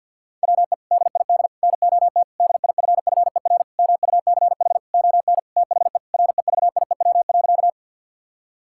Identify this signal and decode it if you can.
Morse code, heard as a keyed tone, about 50 wpm, 700 Hz. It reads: WE DID NOT DIFFER MUCH ON THE REVIEW.
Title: Sentences from Top 1000 Words - Review Part 6 - 50wpm